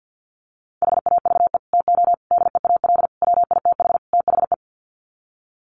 Morse code, heard as a keyed tone, about 48 wpm, 700 Hz. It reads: HAVE TO DEAL WITH THE
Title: Sets of 5 Words Encore2 50wpm